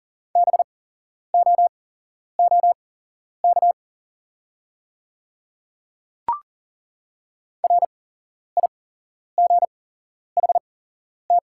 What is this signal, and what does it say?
Morse code, heard as a keyed tone, about 40 wpm, spaced as for 11 wpm, 690 Hz. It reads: BOOK RIGHT
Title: 8x Character Spacing - Top 300 Words - 40wpm